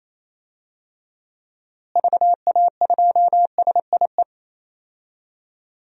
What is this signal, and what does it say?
Morse code, heard as a keyed tone, about 28 wpm, 700 Hz. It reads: VA2SIE
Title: DXCC-100 Call Signs - 6 Characters - Repeated 3x - 28wpm